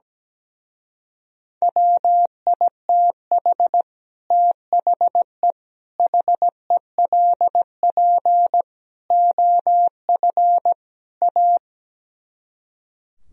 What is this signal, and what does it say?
Morse code, heard as a keyed tone, about 17 wpm, 700 Hz. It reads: WITH THE HELP OF A
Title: Speed-Racing - Sets of 5 Words 17wpm